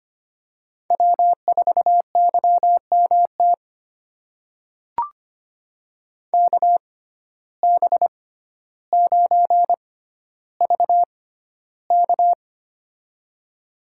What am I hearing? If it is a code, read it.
Morse code, heard as a keyed tone, about 25 wpm, spaced as for 13 wpm, 700 Hz. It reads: W4YMT KB9VK
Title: USA Call Signs - 5 Characters - 6x Character Spacing - 25wpm